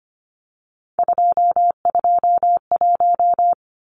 Morse code, heard as a keyed tone, about 25 wpm, 700 Hz. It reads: 221
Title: Numbers - 3-Digit 25wpm